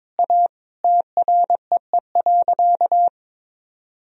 Morse code, heard as a keyed tone, about 22 wpm, 700 Hz. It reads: A TREE.
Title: Sentences from Top 1000 Words - Review Part 1 - 22wpm